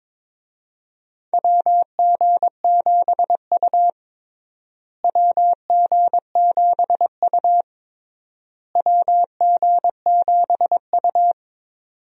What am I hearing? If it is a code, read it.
Morse code, heard as a keyed tone, about 22 wpm, 700 Hz. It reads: WG7U WG7U WG7U
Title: USA Call Signs - 4 Characters - Repeated 3x - 22wpm